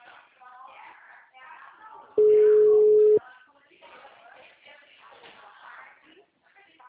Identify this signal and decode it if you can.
Ringback tone (Europe, 425 Hz), with speech.